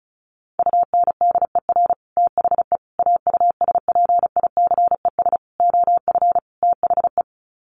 Morse code, heard as a keyed tone, about 34 wpm, 700 Hz. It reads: UNDER THE AUSPICES OF THE
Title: Sets of 5 Words Encore2 35wpm